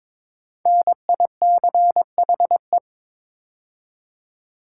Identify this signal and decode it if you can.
Morse code, heard as a keyed tone, about 22 wpm, 700 Hz. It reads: NICHE
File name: 5 Letter Words - First 3 Letters Given - 22wpm